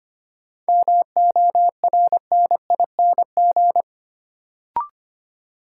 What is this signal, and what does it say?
Morse code, heard as a keyed tone, about 25 wpm, 700 Hz. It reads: MORNING
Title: Sentences from Top 400 Words - New Word Drill - 25wpm